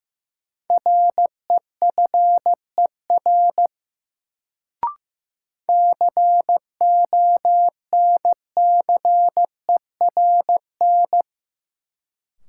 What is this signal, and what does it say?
Morse code, heard as a keyed tone, about 15 wpm, 700 Hz. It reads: REFER CONCERN
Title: Sentences from Top 1000 Words - New Word Drill - 15wpm